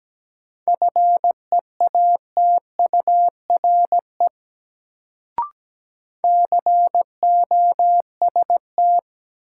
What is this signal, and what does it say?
Morse code, heard as a keyed tone, about 17 wpm, 700 Hz. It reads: FEATURE COST
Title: Sentences from Top 800 Words - New Word Drill - 17wpm